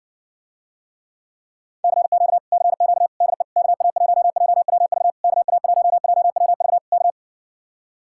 Morse code, heard as a keyed tone, about 60 wpm, 700 Hz. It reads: CQ CQ DE KA1JWV KA1JWV K